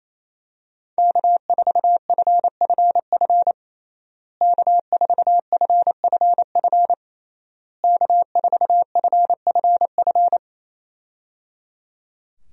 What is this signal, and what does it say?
Morse code, heard as a keyed tone, about 27 wpm, 700 Hz. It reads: K4FFF K4FFF K4FFF
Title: USA Call Signs - 5 Characters - Repeated 3x - 28wpm